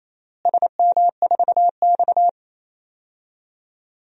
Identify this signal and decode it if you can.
Morse code, heard as a keyed tone, about 27 wpm, 700 Hz. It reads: SM4X
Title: DXCC-100 Call Signs - 4 Characters - Repeated 3x - 28wpm